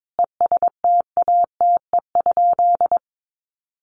Morse code, heard as a keyed tone, about 22 wpm, 700 Hz. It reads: ESTATE?
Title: Sentences from Top 1200 Words 22wpm